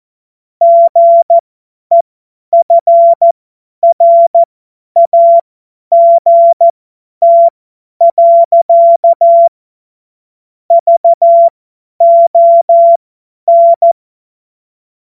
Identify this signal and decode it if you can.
Morse code, heard as a keyed tone, about 14 wpm, spaced as for 10 wpm, 680 Hz. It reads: GEFRAGT. VON
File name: Morsekurs / Witze / Witz 14.November 2023 50 F70